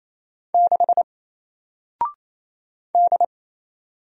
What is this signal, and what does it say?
Morse code, heard as a keyed tone, about 28 wpm, 700 Hz. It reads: 6 D
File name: Single-Letter-Number-28wpm